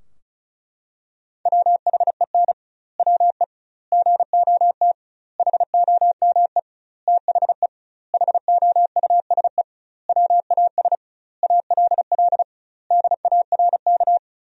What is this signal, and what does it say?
Morse code, heard as a keyed tone, about 34 wpm, 700 Hz. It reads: WHEN WE GOT HOME THE HOUSE WAS ALL DARK
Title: 2x Word-Spacing - Sentences from Top 500 Words - 35wpm